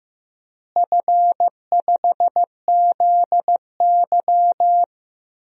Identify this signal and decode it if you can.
Morse code, heard as a keyed tone, about 16 wpm, 700 Hz. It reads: F5ZY
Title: DXCC-100 Call Signs - 4 Characters - Repeated 2x - 15wpm